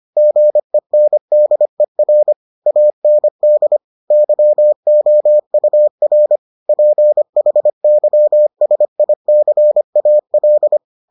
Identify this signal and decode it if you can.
Morse code, heard as a keyed tone, about 25 wpm, 600 Hz. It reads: GENDER AND YOUR PHYSICAL